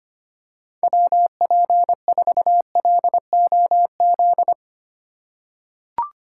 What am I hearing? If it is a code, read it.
Morse code, heard as a keyed tone, about 25 wpm, 700 Hz. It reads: WP4LOZ
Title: USA Call Signs - 6 Characters - 2x Character Spacing - 25wpm